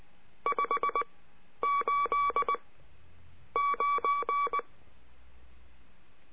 Morse code, heard as a keyed tone, about 19 wpm, spaced as for 8 wpm, 1140 Hz. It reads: 589
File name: cursos / CURSO DE TELEGRAFIA / Telegrafia - Trilha 22